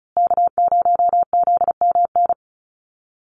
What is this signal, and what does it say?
Morse code, heard as a keyed tone, about 34 wpm, 700 Hz. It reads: K0ZMN
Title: USA Call Signs Repeated 2x 35wpm